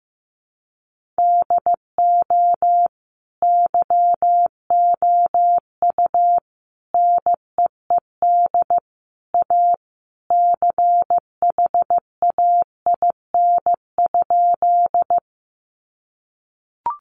Morse code, heard as a keyed tone, about 15 wpm, 700 Hz. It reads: DO YOU NEED A CHAIN?
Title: Sentences from Top 1500 Words 15wpm